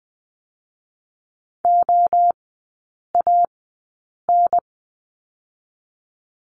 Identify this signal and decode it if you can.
Morse code, heard as a keyed tone, about 20 wpm, 700 Hz. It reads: O A N
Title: 023 Lesson 5e - Letter o - Three Characters 20wpm